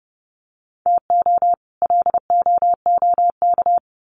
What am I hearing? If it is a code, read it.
Morse code, heard as a keyed tone, about 30 wpm, 700 Hz. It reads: TO LOOK
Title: Sets of 2 Words 30wpm